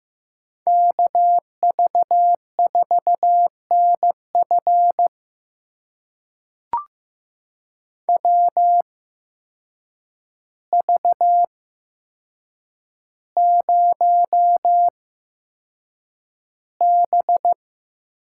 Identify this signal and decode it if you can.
Morse code, heard as a keyed tone, about 15 wpm, spaced as for 7 wpm, 700 Hz. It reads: KV4NF WV0B
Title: USA Call Signs - 8x Character Spacing - 15wpm